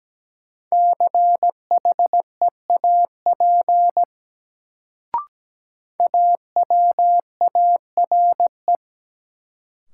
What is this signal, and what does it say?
Morse code, heard as a keyed tone, about 17 wpm, 700 Hz. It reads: CHEAP AWARE
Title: Sentences from Top 1200 Words - New Word Drill Alt - 17wpm